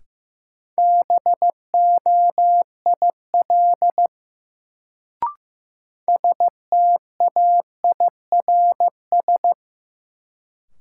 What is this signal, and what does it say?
Morse code, heard as a keyed tone, about 16 wpm, 700 Hz. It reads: BOIL STAIRS
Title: Sets of 5 Words Encore - New Word Drill - 15wpm